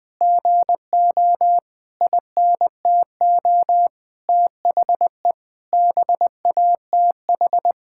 Morse code, heard as a keyed tone, about 20 wpm, 700 Hz. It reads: GO INTO THE BATH
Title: Sentences from Top 1600 Words 20wpm